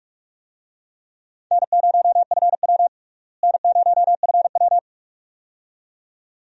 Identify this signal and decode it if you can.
Morse code, heard as a keyed tone, about 46 wpm, 700 Hz. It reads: N0FW N0FW
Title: USA Call Signs - 4 Characters - Repeated 2x - 45wpm